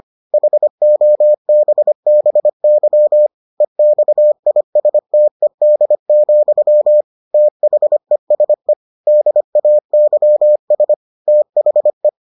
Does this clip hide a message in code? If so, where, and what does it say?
Yes: Morse code from 0.3 to 12.1 s, heard as a keyed tone, about 25 wpm, 600 Hz. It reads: HOBBY EXISTED, THESE DAYS THE